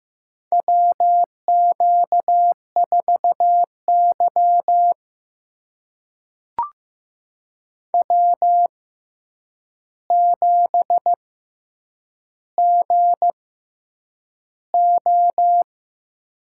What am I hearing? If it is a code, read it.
Morse code, heard as a keyed tone, about 15 wpm, spaced as for 8 wpm, 700 Hz. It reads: WQ4Y W7GO
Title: USA Call Signs - 4 Characters - 6x Character Spacing - 15wpm